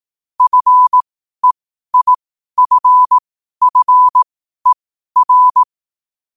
Morse code, heard as a keyed tone, about 18 wpm, spaced as for 12 wpm, 1000 Hz. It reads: FEIFFER
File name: qotdfiles / 12wpm